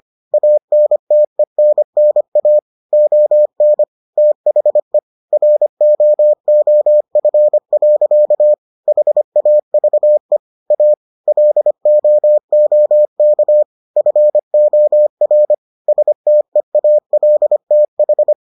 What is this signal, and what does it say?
Morse code, heard as a keyed tone, about 25 wpm, 600 Hz. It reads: ANTENNA ON THE ROOF. HAVE A LOOK FOR STEALTH